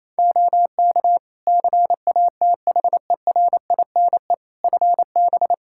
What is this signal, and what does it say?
Morse code, heard as a keyed tone, about 28 wpm, 700 Hz. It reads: OK CATHERINE FB
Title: Practice QSO - Exchange 2 - Info - 28wpm